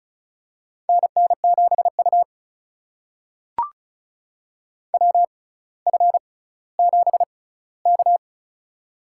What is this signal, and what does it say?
Morse code, heard as a keyed tone, about 36 wpm, spaced as for 18 wpm, 700 Hz. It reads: NN7U WF7K
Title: USA Call Signs - 4 Characters - 6x Character Spacing - 35wpm